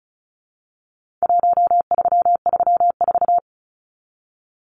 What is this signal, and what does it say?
Morse code, heard as a keyed tone, about 34 wpm, 700 Hz. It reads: 1334
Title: Numbers - Contest Style - 4-Digit 35wpm